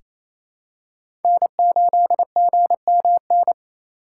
Morse code, heard as a keyed tone, about 28 wpm, 700 Hz. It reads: N8GMN
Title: USA Call Signs - 5 Characters - 4x Character Spacing - 28wpm